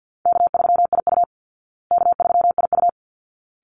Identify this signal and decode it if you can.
Morse code, heard as a keyed tone, about 50 wpm, 700 Hz. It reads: K3IU K3IU
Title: Easy Call Signs Repeated 2x - 50wpm